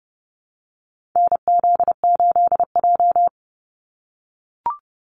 Morse code, heard as a keyed tone, about 30 wpm, 700 Hz. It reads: NZ8J
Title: Call Signs - 30wpm